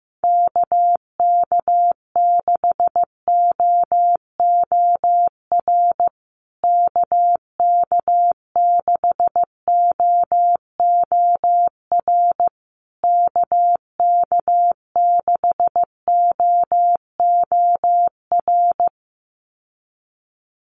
Morse code, heard as a keyed tone, about 15 wpm, 700 Hz. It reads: KK6OOR KK6OOR KK6OOR
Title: USA Call Signs Repeated 3x 15wpm